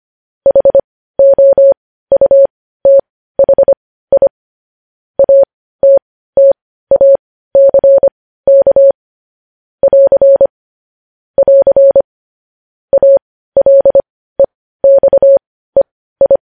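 Morse code, heard as a keyed tone, about 25 wpm, spaced as for 15 wpm, 550 Hz. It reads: HOUTHI ATTACK <AR> <AR> ALEXEI
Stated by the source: Morse news headlines sent at 15 WPM, beginning with vvv.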